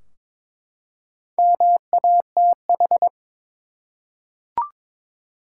Morse code, heard as a keyed tone, about 22 wpm, 700 Hz. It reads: MATH
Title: Four Letter Words - 22wpm